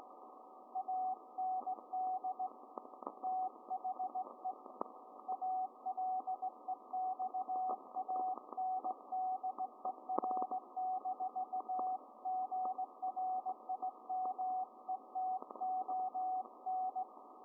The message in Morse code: AND THE ALEXANDER-GRIMETON